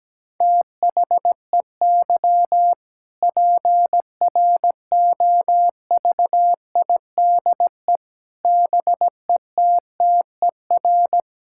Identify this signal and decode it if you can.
Morse code, heard as a keyed tone, about 17 wpm, 700 Hz. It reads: THEY PROVIDE BETTER
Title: Sentences from Top 1000 Words - Review Part 5 - 17wpm